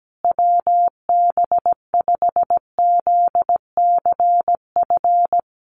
Morse code, heard as a keyed tone, about 17 wpm, 700 Hz. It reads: WB5ZCF
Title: USA Call Signs Repeated 3x 17wpm